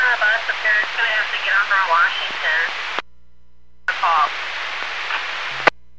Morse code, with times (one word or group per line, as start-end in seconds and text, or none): none